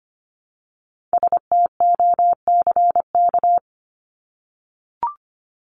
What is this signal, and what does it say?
Morse code, heard as a keyed tone, about 25 wpm, 700 Hz. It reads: STOCK
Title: Top 501-750 Words 25wpm